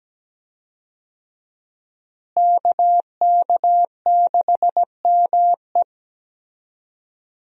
Morse code, heard as a keyed tone, about 17 wpm, 700 Hz. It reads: KK6ME
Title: USA Call Signs - 5 Characters - Repeated 2x - 17wpm